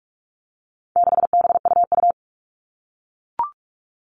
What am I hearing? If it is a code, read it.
Morse code, heard as a keyed tone, about 44 wpm, 700 Hz. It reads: 6BUU
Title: All Cut Numbers - 4-Digit 45wpm